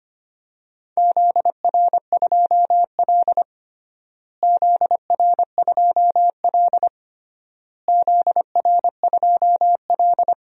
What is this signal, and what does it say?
Morse code, heard as a keyed tone, about 25 wpm, 700 Hz. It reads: ZR2L ZR2L ZR2L
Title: DXCC-100 Call Signs - 4 Characters - Repeated 3x - 25wpm